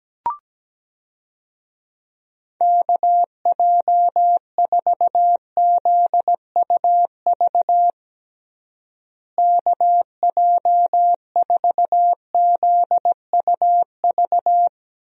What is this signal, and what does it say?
Morse code, heard as a keyed tone, about 17 wpm, 700 Hz. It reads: KJ4ZUV KJ4ZUV
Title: USA Call Signs - 6 Characters - Repeated 2x - 17wpm